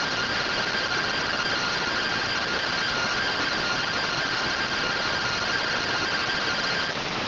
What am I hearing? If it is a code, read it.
Two-tone teleprinter signal, about 170 Hz shift.